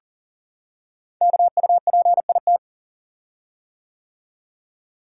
Morse code, heard as a keyed tone, about 40 wpm, 700 Hz. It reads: KUPIT
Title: Code Groups - 5 Letter:Numbers - 40wpm